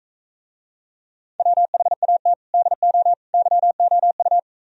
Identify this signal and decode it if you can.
Morse code, heard as a keyed tone, about 45 wpm, 700 Hz. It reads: WHAT DO YOU